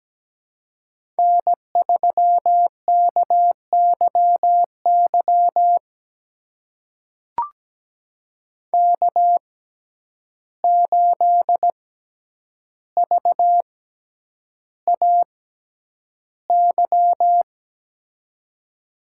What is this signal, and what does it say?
Morse code, heard as a keyed tone, about 17 wpm, spaced as for 9 wpm, 700 Hz. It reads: N3KYY K8VAY
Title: USA Call Signs - 6x Character Spacing - 17wpm